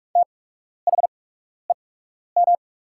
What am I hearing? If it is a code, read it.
Morse code, heard as a keyed tone, about 44 wpm, spaced as for 12 wpm, 700 Hz. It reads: THEM
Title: 8x Character Spacing - Top 200 Words - 45wpm